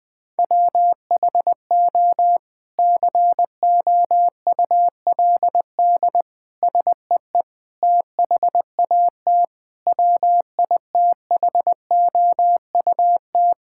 Morse code, heard as a keyed tone, about 21 wpm, 700 Hz. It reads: WHO COULD SEE THAT WITHOUT
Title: Sentences from Top 1600 Words 20wpm